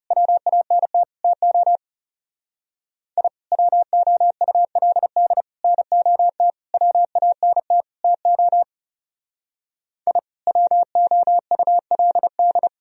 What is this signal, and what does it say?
Morse code, heard as a keyed tone, about 35 wpm, 700 Hz. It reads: WANT TO I WOULD NOT WANT TO I WOULD
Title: Speed-Racing - Sets of 5 Words 30wpm